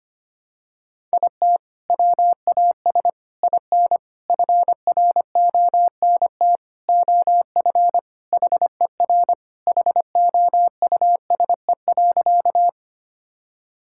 Morse code, heard as a keyed tone, about 25 wpm, 700 Hz. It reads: IT WAS IN FRONT OF HER HOUSE.
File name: practice / 25wpm / Sentences with 4-Word Phrases - 25wpm